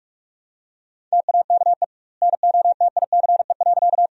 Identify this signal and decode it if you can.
Morse code, heard as a keyed tone, about 48 wpm, 700 Hz. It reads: TAKE NOTICE.
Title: 2x Word-Spacing - Sentences from Top 400 Words - 45wpm